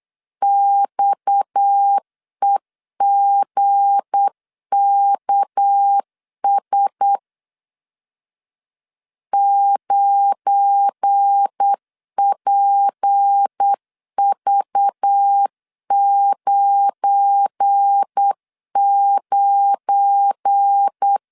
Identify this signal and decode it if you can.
Morse code, heard as a keyed tone, about 8 wpm, 800 Hz. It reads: XEGKS 9PV99
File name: LEZIONI / 7 GF9Z / 7 GF9 8wpm 3545